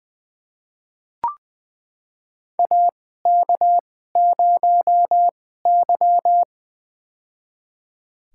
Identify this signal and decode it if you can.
Morse code, heard as a keyed tone, about 20 wpm, spaced as for 15 wpm, 700 Hz. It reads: AK0Y
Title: USA Call Signs - 4 Characters - 2x Character Spacing - 20wpm